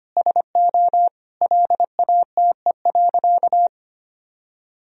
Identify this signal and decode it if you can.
Morse code, heard as a keyed tone, about 25 wpm, 700 Hz. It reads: SO LATE.
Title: Sentences from Top 1000 Words - Review Part 3 - 25wpm